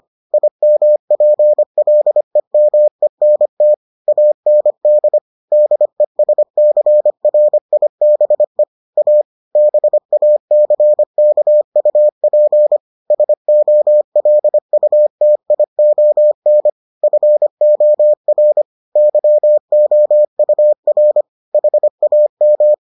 Morse code, heard as a keyed tone, about 25 wpm, 600 Hz. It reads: IMPLEMENT AND DESCRIBE A BACKUP SOLUTION FOR YOUR HAM